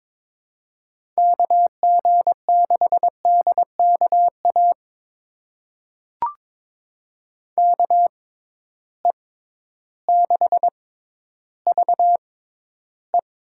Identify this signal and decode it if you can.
Morse code, heard as a keyed tone, about 22 wpm, spaced as for 12 wpm, 700 Hz. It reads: KG6DKA KE6VE
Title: USA Call Signs - 6 Characters - 6x Character Spacing - 22wpm